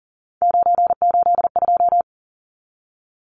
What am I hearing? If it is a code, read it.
Morse code, heard as a keyed tone, about 39 wpm, 700 Hz. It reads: 982
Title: Numbers - 3-Digit 40wpm